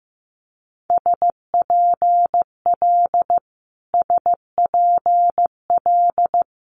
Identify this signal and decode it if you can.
Morse code, heard as a keyed tone, about 15 wpm, 700 Hz. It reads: SPL SPL
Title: Multi-letter Phonemes Repeated 3x 15wpm